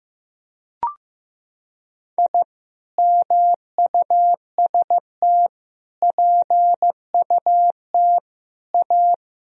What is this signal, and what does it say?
Morse code, heard as a keyed tone, about 15 wpm, 700 Hz. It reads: I MUST PUT A
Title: Sentences from Top 1000 Words - Review Part 1 - 15wpm